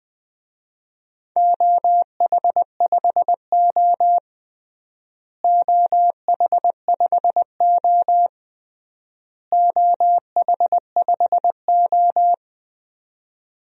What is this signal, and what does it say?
Morse code, heard as a keyed tone, about 20 wpm, 700 Hz. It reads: OH5O OH5O OH5O
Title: DXCC-100 Call Signs - 4 Characters - Repeated 3x - 20wpm